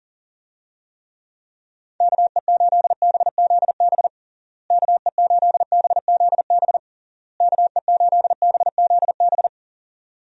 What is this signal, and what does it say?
Morse code, heard as a keyed tone, about 37 wpm, 700 Hz. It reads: KE8BZB KE8BZB KE8BZB
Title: USA Call Signs - 6 Characters - Repeated 3x - 40wpm